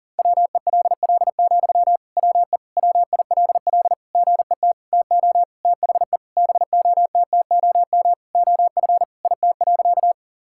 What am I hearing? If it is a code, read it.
Morse code, heard as a keyed tone, about 40 wpm, 700 Hz. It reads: WELL, WE WILL GET TO THE BOTTOM OF IT.
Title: Sentences from Top 1000 Words - Review Part 6 - 40wpm